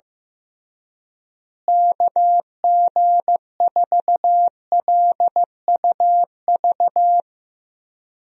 Morse code, heard as a keyed tone, about 16 wpm, 700 Hz. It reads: KG4LUV
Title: USA Call Signs - 6 Characters - Repeated 3x - 15wpm